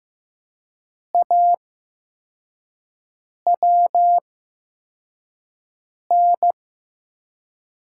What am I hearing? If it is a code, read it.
Morse code, heard as a keyed tone, about 15 wpm, spaced as for 4 wpm, 700 Hz. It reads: AWN
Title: USA Call Signs - 6 Characters - 8x Character Spacing - 15wpm